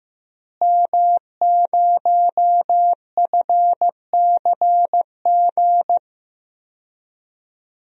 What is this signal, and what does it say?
Morse code, heard as a keyed tone, about 15 wpm, 700 Hz. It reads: M0FCG
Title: DXCC-100 Call Signs - 5 Characters - Repeated 2x - 15wpm